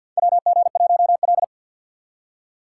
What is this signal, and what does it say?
Morse code, heard as a keyed tone, about 51 wpm, 700 Hz. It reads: WG1L
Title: USA Call Signs - 4 Characters - 6x Character Spacing - 50wpm